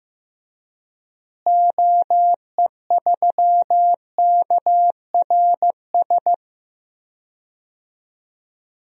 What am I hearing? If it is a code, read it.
Morse code, heard as a keyed tone, about 16 wpm, 700 Hz. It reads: OE3KRS
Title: DXCC-100 Call Signs - 6 Characters - Repeated 2x - 15wpm